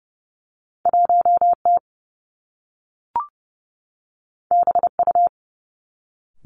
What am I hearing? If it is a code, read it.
Morse code, heard as a keyed tone, about 30 wpm, 700 Hz. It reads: 1 T B U
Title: Two Challenging Letter-Number Combinations 30wpm